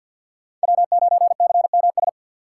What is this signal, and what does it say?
Morse code, heard as a keyed tone, about 46 wpm, 700 Hz. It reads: W9CMS
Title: USA Call Signs - 5 Characters - 4x Character Spacing - 50wpm